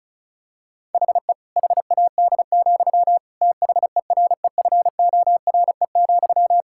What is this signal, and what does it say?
Morse code, heard as a keyed tone, about 36 wpm, 700 Hz. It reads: HE HAD, THEREFORE,
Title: Sentences from Top 1000 Words - Review Part 5 - 35wpm